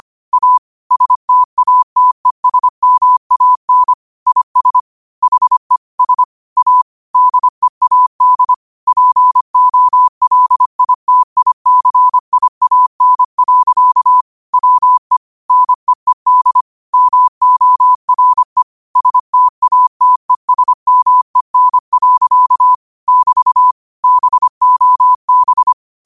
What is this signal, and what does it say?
Morse code, heard as a keyed tone, about 25 wpm, 1000 Hz. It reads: A STATESMAN IS HE S A DEAD POLITICIAN. WE NEED MORE STATESMEN. <BT> BOB